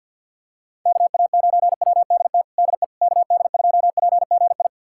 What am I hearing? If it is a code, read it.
Morse code, heard as a keyed tone, about 51 wpm, 700 Hz. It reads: KA9WDT DE KD2PGI
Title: USA Callsign DE Callsign - 6 Characters - 50wpm